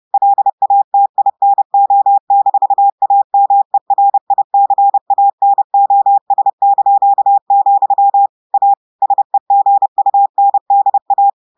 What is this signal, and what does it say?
Morse code, heard as a keyed tone, about 30 wpm, 800 Hz. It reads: LATINO-AMERICANOS), A SEGUNDA
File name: audios / portugais / portugais texte en clair 1 vitesse 30 mots minutes